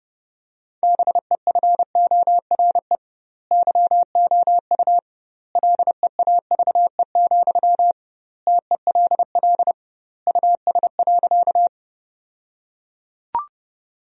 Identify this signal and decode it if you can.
Morse code, heard as a keyed tone, about 32 wpm, 700 Hz. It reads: BEFORE YOU LEAVE, TELL US.
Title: 2x Word-Spacing - Sentences from Top 400 Words - 30wpm